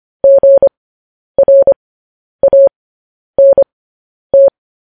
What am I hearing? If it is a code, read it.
Morse code, heard as a keyed tone, about 25 wpm, spaced as for 10 wpm, 550 Hz. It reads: GRANT
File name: podcast / News headlines-1x-25x10